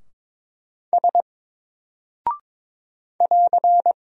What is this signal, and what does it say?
Morse code, heard as a keyed tone, about 22 wpm, 700 Hz. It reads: S <AR>
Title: Single Character (Letter:Number:Punctuation:Prosign) Alt 22wpm